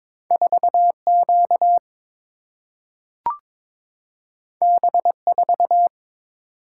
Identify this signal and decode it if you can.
Morse code, heard as a keyed tone, about 22 wpm, 700 Hz. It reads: 4Q B4
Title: Single-Letter Single-Number 22wpm